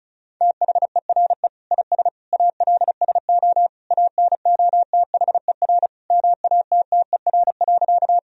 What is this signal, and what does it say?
Morse code, heard as a keyed tone, about 36 wpm, 700 Hz. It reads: THERE IS ALSO ANOTHER MATTER.
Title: Sentences from Top 1000 Words - Review Part 1 - 35wpm